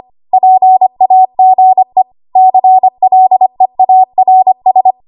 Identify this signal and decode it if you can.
Morse code, heard as a keyed tone, about 25 wpm, 750 Hz. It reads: PAGE CLEARS
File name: Archive / 25 WPM / 25WPM